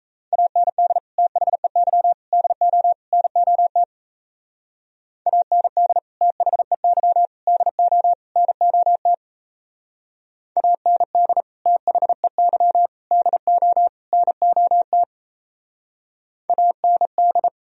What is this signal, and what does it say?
Morse code, heard as a keyed tone, about 34 wpm, 700 Hz. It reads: AND THEY DO NOT AND THEY DO NOT AND THEY DO NOT AND
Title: Speed-Racing - Sets of 4 Words 28wpm